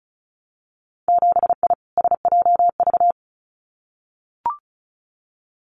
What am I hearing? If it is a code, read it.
Morse code, heard as a keyed tone, about 36 wpm, 700 Hz. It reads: 7I SJV
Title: Winter Field Day Exchanges Encore 35wpm